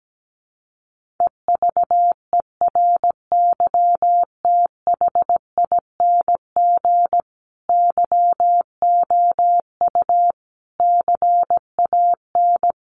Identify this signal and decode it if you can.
Morse code, heard as a keyed tone, about 17 wpm, 700 Hz. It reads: EVERYTHING YOU CAN
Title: Sentences from Top 1400 Words 17wpm